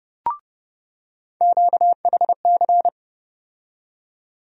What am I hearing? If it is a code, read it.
Morse code, heard as a keyed tone, about 30 wpm, 700 Hz. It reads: QHC
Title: USA Call Sign Postfixes - 3 Letters 30wpm